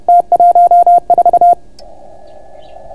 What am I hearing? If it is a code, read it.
Morse code, heard as a keyed tone, about 30 wpm, 690 Hz. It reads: T14